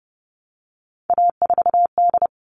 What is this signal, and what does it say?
Morse code, heard as a keyed tone, about 29 wpm, 700 Hz. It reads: A4D
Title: All Cut Numbers - 3-Digit 30wpm